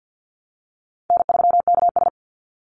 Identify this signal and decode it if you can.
Morse code, heard as a keyed tone, about 50 wpm, 700 Hz. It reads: N3KS
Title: Easy Call Signs - 50wpm